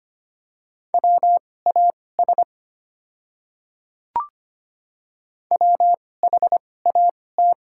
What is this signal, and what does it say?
Morse code, heard as a keyed tone, about 25 wpm, spaced as for 18 wpm, 700 Hz. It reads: WAS WHAT
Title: 2x Character Spacing - Top 100 Words - 25wpm